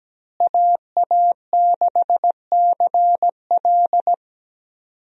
Morse code, heard as a keyed tone, about 17 wpm, 700 Hz. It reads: AA6CL
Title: USA Call Signs - 5 Characters - Repeated 2x - 17wpm